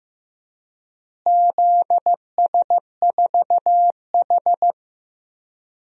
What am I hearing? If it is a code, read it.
Morse code, heard as a keyed tone, about 16 wpm, 700 Hz. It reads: ZS4H
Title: DXCC-100 Call Signs - 4 Characters - Repeated 3x - 15wpm